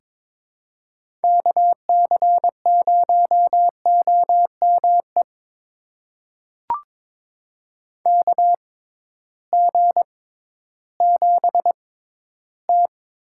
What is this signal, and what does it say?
Morse code, heard as a keyed tone, about 22 wpm, spaced as for 13 wpm, 700 Hz. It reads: KC0OME KG7T
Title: USA Call Signs - 6 Characters - 6x Character Spacing - 22wpm